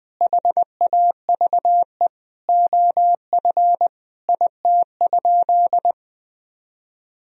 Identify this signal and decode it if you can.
Morse code, heard as a keyed tone, about 20 wpm, 700 Hz. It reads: HAVE OF IT?
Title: Sentences from Top 1000 Words - Review Part 3 - 20wpm